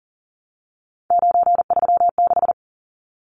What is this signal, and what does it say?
Morse code, heard as a keyed tone, about 40 wpm, 700 Hz. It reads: 936
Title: Numbers - 3-Digit 40wpm